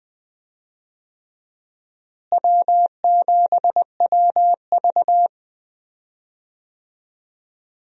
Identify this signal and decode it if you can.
Morse code, heard as a keyed tone, about 20 wpm, 700 Hz. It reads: W7WV
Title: USA Call Signs - 4 Characters - Repeated 3x - 20wpm